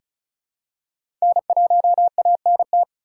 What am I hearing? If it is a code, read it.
Morse code, heard as a keyed tone, about 36 wpm, 700 Hz. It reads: N1ANT
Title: USA Call Signs - 5 Characters - 2x Character Spacing - 35wpm